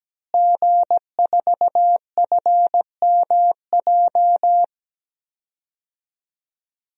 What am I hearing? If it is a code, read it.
Morse code, heard as a keyed tone, about 17 wpm, 700 Hz. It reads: G4FMJ
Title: DXCC-100 Call Signs - 5 Characters - Repeated 3x - 17wpm